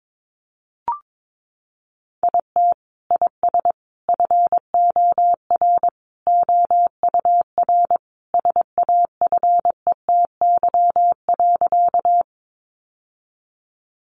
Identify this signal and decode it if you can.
Morse code, heard as a keyed tone, about 22 wpm, 700 Hz. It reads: IT IS FOR OUR SAFETY.
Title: Sentences from Top 700 Words 22wpm